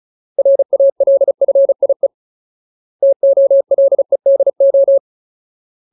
Morse code, heard as a keyed tone, about 34 wpm, 550 Hz. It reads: RALFIE TOLEDO
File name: Ultimate Word List / 35WPM Ultimate